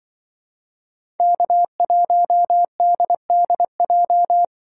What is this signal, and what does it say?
Morse code, heard as a keyed tone, about 23 wpm, 700 Hz. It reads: K1DDJ
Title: Speed Racing - USA 1x3 Call Signs 20wpm